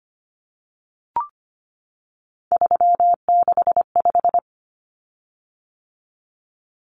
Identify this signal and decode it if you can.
Morse code, heard as a keyed tone, about 25 wpm, 700 Hz. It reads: 365
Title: Common Cut Numbers - 3-Digit 25wpm